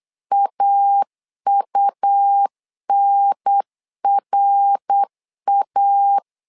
Morse code, heard as a keyed tone, about 8 wpm, 800 Hz. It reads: AUNRA
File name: LEZIONI / 4 Q1S 8wpm 2710